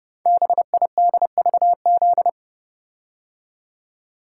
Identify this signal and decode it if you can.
Morse code, heard as a keyed tone, about 30 wpm, 700 Hz. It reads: BIDVZ
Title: Code Groups - 5 Letter:Numbers - 30wpm